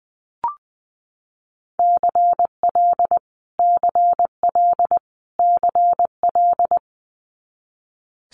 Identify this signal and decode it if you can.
Morse code, heard as a keyed tone, about 20 wpm, 700 Hz. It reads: CL CL CL
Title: Multi-letter Phonemes Repeated 3x 20wpm